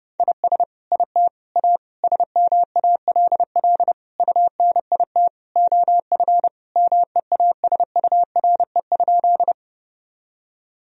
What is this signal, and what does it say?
Morse code, heard as a keyed tone, about 32 wpm, 700 Hz. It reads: IS IT A SMALL UNIT OF MEASURE?
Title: Sentences from Top 1000 Words - Review Part 2 - 30wpm